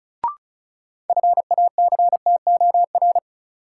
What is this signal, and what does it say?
Morse code, heard as a keyed tone, about 34 wpm, 700 Hz. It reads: FACTOR
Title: Sentences from Top 1300 Words - New Word Drill Alt - 35wpm